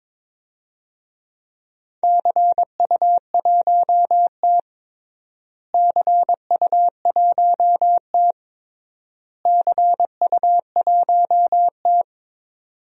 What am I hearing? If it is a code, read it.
Morse code, heard as a keyed tone, about 22 wpm, 700 Hz. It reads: CU1T CU1T CU1T
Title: DXCC-100 Call Signs - 4 Characters - Repeated 3x - 22wpm